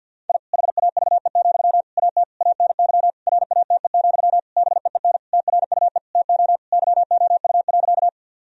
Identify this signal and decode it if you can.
Morse code, heard as a keyed tone, about 50 wpm, 700 Hz. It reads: I HAVE, AT ANY RATE, BEEN TRUE TO YOU.
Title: Sentences from Top 1000 Words - Review Part 4 - 50wpm